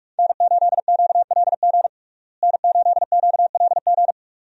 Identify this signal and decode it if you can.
Morse code, heard as a keyed tone, about 42 wpm, 700 Hz. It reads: N8QLG N8QLG
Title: USA Call Signs - 5 Characters - Repeated 2x - 45wpm